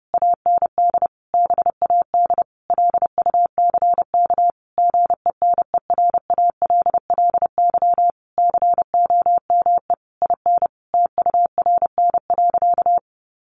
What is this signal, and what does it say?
Morse code, heard as a keyed tone, about 30 wpm, 700 Hz. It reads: AND BAD LUCK GENERALLY COME IN TURN.
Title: Sentences from Top 1500 Words 30wpm